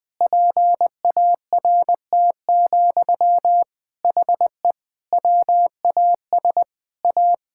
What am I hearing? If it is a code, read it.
Morse code, heard as a keyed tone, about 20 wpm, 700 Hz. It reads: PART, HE WAS A
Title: Sentences from Top 1000 Words - Review Part 5 - 20wpm